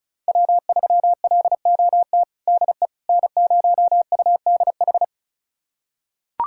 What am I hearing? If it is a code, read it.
Morse code, heard as a keyed tone, about 36 wpm, 690 Hz. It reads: W3LOT DE N0UDH
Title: USA Callsign DE Callsign - 5 Characters - 35wpm